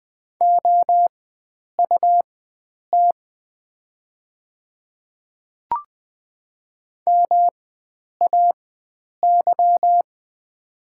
Morse code, heard as a keyed tone, about 20 wpm, spaced as for 9 wpm, 700 Hz. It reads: OUT MAY
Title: 4x Character Spacing - Top 100 Words - 20wpm